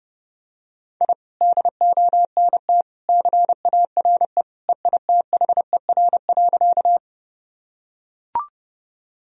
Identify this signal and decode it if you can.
Morse code, heard as a keyed tone, about 30 wpm, 700 Hz. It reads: I DONT CARE EITHER.
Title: Sentences from Top 1000 Words - Review Part 4 - 30wpm